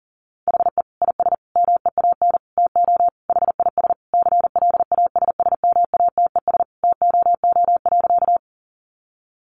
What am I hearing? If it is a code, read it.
Morse code, heard as a keyed tone, about 40 wpm, 700 Hz. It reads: HE IS MEAN TO HIS CLASSMATES TOO.
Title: Additional Easy Phrases 40wpm